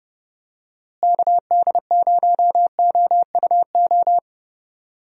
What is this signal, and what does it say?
Morse code, heard as a keyed tone, about 30 wpm, 700 Hz. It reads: KD0OUO
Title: USA Call Signs - 6 Characters - Repeated 3x - 30wpm